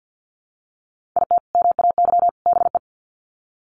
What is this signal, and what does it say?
Morse code, heard as a keyed tone, about 48 wpm, 700 Hz. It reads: IT MAY BE
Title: Sets of 3 Words 50wpm